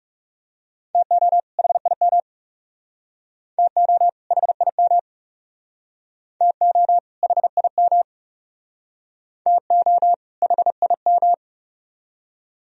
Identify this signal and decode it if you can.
Morse code, heard as a keyed tone, about 38 wpm, 700 Hz. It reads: TO HIM TO HIM TO HIM TO HIM